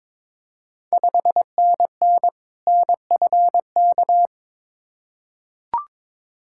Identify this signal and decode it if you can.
Morse code, heard as a keyed tone, about 22 wpm, 700 Hz. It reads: 5NN NFK
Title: Ontario QSO Party Exchange 22wpm